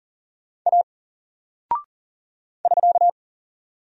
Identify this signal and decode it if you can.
Morse code, heard as a keyed tone, about 40 wpm, 700 Hz. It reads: A <SK>
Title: Single Character (Letter:Number:Punctuation:Prosign) Alt 40wpm